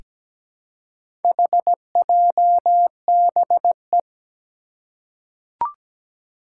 Morse code, heard as a keyed tone, about 17 wpm, 700 Hz. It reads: HJBE
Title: Four Challenging Letter-Number Combinations 17wpm